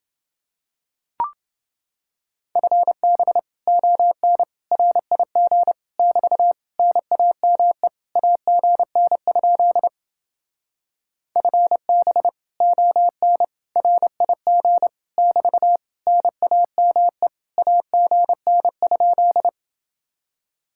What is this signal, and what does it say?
Morse code, heard as a keyed tone, about 28 wpm, 700 Hz. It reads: FB ON RIG <BT> NAME AGN? FB ON RIG <BT> NAME AGN?